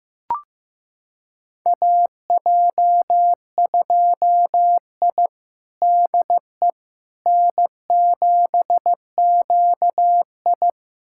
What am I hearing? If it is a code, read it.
Morse code, heard as a keyed tone, about 15 wpm, 700 Hz. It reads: AJ2I DE N7QI
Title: USA Callsign DE Callsign - 4 Characters - 15wpm